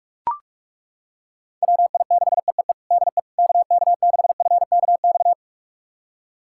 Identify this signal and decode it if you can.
Morse code, heard as a keyed tone, about 41 wpm, 700 Hz. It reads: WI6EEE DE KK6FKX
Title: USA Callsign DE Callsign - 6 Characters - 45wpm